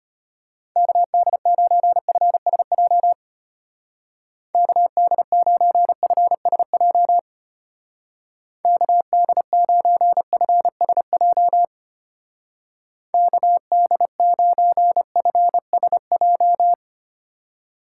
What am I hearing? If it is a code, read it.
Morse code, heard as a keyed tone, about 32 wpm, 700 Hz. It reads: KD9FSJ KD9FSJ KD9FSJ KD9FSJ